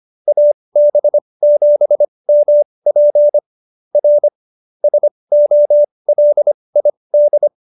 Morse code, heard as a keyed tone, about 25 wpm, spaced as for 20 wpm, 590 Hz. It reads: AB7MP R SOLID